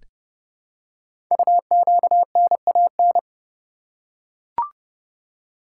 Morse code, heard as a keyed tone, about 30 wpm, 700 Hz. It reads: UQNAN
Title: Code Groups - 5 Letters - 30wpm